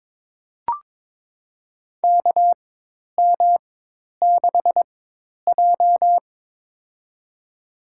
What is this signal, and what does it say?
Morse code, heard as a keyed tone, about 22 wpm, spaced as for 10 wpm, 700 Hz. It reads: KM6J
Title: USA Call Signs - 4 Characters - 4x Character Spacing - 22wpm